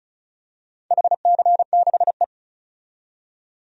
Morse code, heard as a keyed tone, about 36 wpm, 700 Hz. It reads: HC6E
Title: DXCC-100 Call Signs - 4 Characters - 35wpm